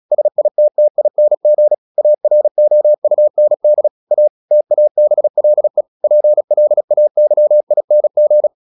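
Morse code, heard as a keyed tone, about 36 wpm, 600 Hz. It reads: SITTING AROUND A TABLE PLAYING
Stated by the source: Morse practice at 36 wpm, text of Ken Follett's novel Never, from CW med Gnister.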